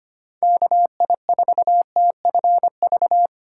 Morse code, heard as a keyed tone, about 25 wpm, 700 Hz. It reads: KI4TFV
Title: USA Call Signs - 6 Characters - 3x Character Spacing - 25wpm